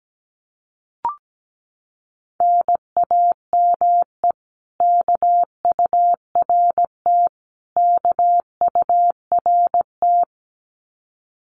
Morse code, heard as a keyed tone, about 17 wpm, 700 Hz. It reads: NAME KURT KURT
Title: Common QSO Phrases Encore 17wpm